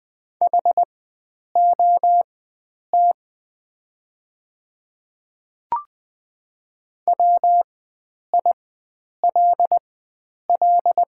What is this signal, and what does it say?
Morse code, heard as a keyed tone, about 20 wpm, spaced as for 9 wpm, 700 Hz. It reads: HOT WILL
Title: 4x Character Spacing - Top 100 Words - 20wpm